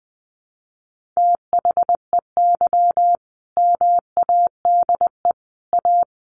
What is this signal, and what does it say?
Morse code, heard as a keyed tone, about 20 wpm, 700 Hz. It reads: THEY MADE A
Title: Sentences from Top 1300 Words 20wpm